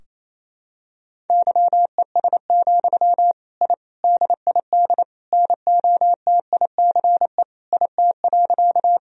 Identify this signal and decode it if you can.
Morse code, heard as a keyed tone, about 28 wpm, 700 Hz. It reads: YES, I DID NOTICE IT.
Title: Sentences from Top 1000 Words - Review Part 2 - 28wpm